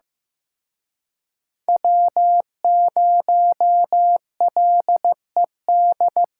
Morse code, heard as a keyed tone, about 18 wpm, 700 Hz. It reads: W0LED
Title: Speed Racing - USA 1x3 Call Signs 15wpm